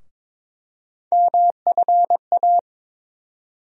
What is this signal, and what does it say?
Morse code, heard as a keyed tone, about 22 wpm, 700 Hz. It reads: MFA
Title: USA Call Sign Postfixes - 3 Letters 22wpm